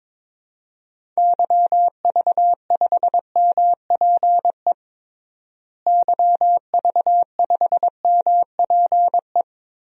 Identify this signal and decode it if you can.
Morse code, heard as a keyed tone, about 22 wpm, 700 Hz. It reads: YV5MPE YV5MPE
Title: DXCC-100 Call Signs - 6 Characters - Repeated 3x - 22wpm